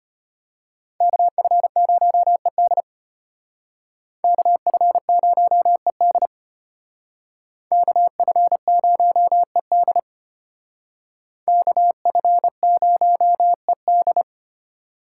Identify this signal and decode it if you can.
Morse code, heard as a keyed tone, about 33 wpm, 700 Hz. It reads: KF0ED KF0ED KF0ED KF0ED